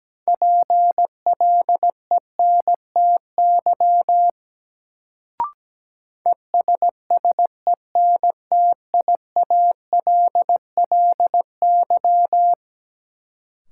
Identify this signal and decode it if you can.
Morse code, heard as a keyed tone, about 17 wpm, 700 Hz. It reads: PLENTY ESSENTIALLY
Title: Sentences from Top 1300 Words - New Word Drill Alt - 17wpm